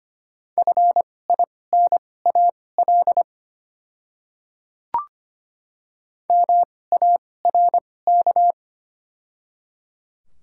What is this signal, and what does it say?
Morse code, heard as a keyed tone, about 25 wpm, spaced as for 18 wpm, 700 Hz. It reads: FINAL MARK
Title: 2x Character Spacing - Top 500 Words - 25wpm